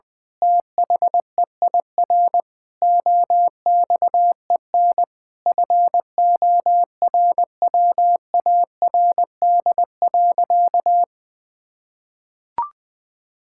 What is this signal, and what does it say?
Morse code, heard as a keyed tone, about 21 wpm, 700 Hz. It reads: THEIR OXEN FORWARD.
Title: Pangram Challenge v2 - 20wpm